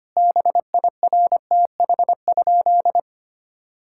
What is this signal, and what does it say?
Morse code, heard as a keyed tone, about 25 wpm, 700 Hz. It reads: BIRTH?
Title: Sentences from Top 1000 Words - Review Part 5 - 25wpm